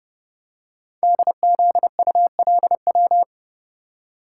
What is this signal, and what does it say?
Morse code, heard as a keyed tone, about 30 wpm, 690 Hz. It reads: DZULW
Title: Code Groups - 5 Letters - 30wpm